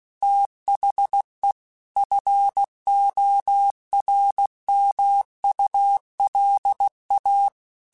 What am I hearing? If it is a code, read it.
Morse code, heard as a keyed tone, about 16 wpm, 800 Hz. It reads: THE FORMULA
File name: Technical Subjects / Technical Subjects-mp3s / XC @15